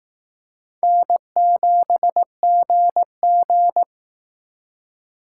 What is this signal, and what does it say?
Morse code, heard as a keyed tone, about 18 wpm, 700 Hz. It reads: N7GG
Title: Speed Racing - USA 1x2 Call Signs 15wpm